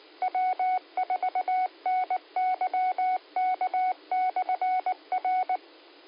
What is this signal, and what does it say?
Morse code, heard as a keyed tone, about 19 wpm, 740 Hz. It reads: W4NYK/R
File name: qsorec SimplexLogic